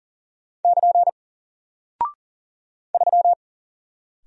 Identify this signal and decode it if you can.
Morse code, heard as a keyed tone, about 39 wpm, 700 Hz. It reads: <KN> 3
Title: Single Character (Letter:Number:Punctuation:Prosign) 40wpm